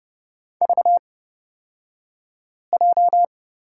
Morse code, heard as a keyed tone, about 31 wpm, spaced as for 5 wpm, 700 Hz. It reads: VJ
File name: Lesson 38d - Letter q - Two Characters 30wpm